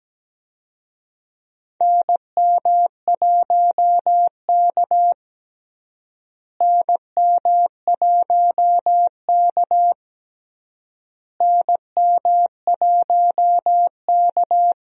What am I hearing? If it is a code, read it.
Morse code, heard as a keyed tone, about 17 wpm, 700 Hz. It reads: NM1K NM1K NM1K
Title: USA Call Signs - 4 Characters - Repeated 3x - 17wpm